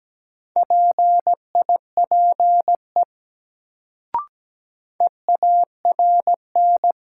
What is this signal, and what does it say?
Morse code, heard as a keyed tone, about 17 wpm, 700 Hz. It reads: PIPE EARN
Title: Sentences from Top 1500 Words - New Word Drill Alt - 17wpm